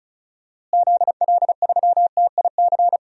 Morse code, heard as a keyed tone, about 34 wpm, 700 Hz. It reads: ZL3TIC
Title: DXCC-100 Call Signs - 6 Characters - Repeated 2x - 35wpm